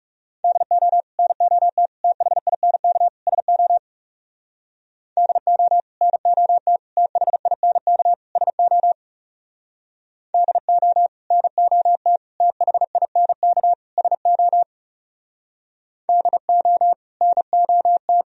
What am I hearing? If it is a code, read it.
Morse code, heard as a keyed tone, about 37 wpm, 700 Hz. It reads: DO NOT THINK SO DO NOT THINK SO DO NOT THINK SO DO NOT
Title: Speed-Racing - Sets of 4 Words 30wpm